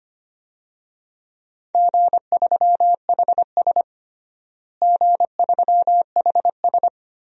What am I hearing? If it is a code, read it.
Morse code, heard as a keyed tone, about 25 wpm, 700 Hz. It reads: G3HS G3HS
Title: DXCC-100 Call Signs - 4 Characters - Repeated 3x - 25wpm